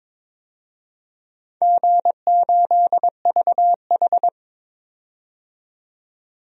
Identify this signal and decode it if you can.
Morse code, heard as a keyed tone, about 22 wpm, 700 Hz. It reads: G8VH
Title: DXCC-100 Call Signs - 4 Characters - 22wpm